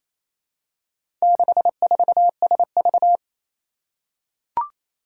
Morse code, heard as a keyed tone, about 27 wpm, 700 Hz. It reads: 64SV
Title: Four Challenging Letter-Number Combinations 28wpm